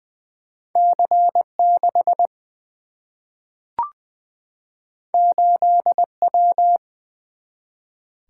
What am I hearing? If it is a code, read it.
Morse code, heard as a keyed tone, about 20 wpm, 700 Hz. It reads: C 6 8 W
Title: Single-Letter Single-Number 20wpm